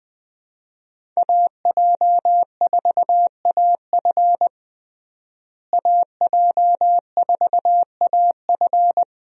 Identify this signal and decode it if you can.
Morse code, heard as a keyed tone, about 20 wpm, 700 Hz. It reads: AJ4AF AJ4AF
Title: USA Call Signs - 5 Characters - Repeated 3x - 20wpm